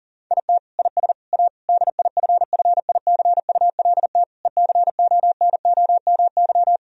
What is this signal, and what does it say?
Morse code, heard as a keyed tone, about 40 wpm, 700 Hz. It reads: IT IS A DIFFICULT ECONOMY
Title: Sentences from Top 1000 Words - Review Part 3 - 40wpm